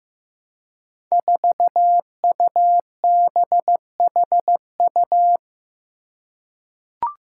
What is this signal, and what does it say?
Morse code, heard as a keyed tone, about 15 wpm, 700 Hz. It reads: 4UBHU
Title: Five Challenging Letter-Number Combinations 15wpm